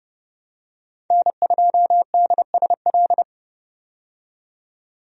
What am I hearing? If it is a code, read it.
Morse code, heard as a keyed tone, about 30 wpm, 700 Hz. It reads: N2DSL
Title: USA Call Signs - 5 Characters - 30wpm